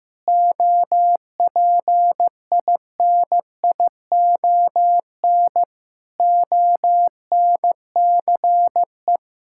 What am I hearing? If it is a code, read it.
Morse code, heard as a keyed tone, about 15 wpm, 700 Hz. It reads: OPINION ONCE
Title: Sentences from Top 1600 Words 15wpm